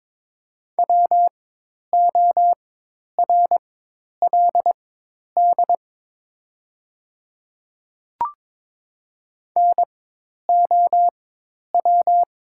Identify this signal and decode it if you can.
Morse code, heard as a keyed tone, about 22 wpm, spaced as for 10 wpm, 700 Hz. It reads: WORLD NOW
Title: 4x Character Spacing - Top 300 Words - 22wpm